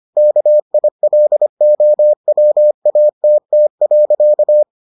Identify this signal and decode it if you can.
Morse code, heard as a keyed tone, about 25 wpm, 600 Hz. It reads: KILOWATT.